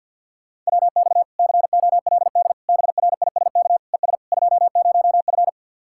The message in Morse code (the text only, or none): WX COLD BRISK ES 20F